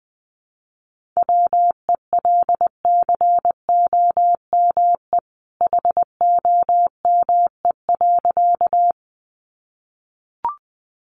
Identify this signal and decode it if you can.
Morse code, heard as a keyed tone, about 20 wpm, 700 Hz. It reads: WELCOME HOME.
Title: A Sound Of Thunder 20wpm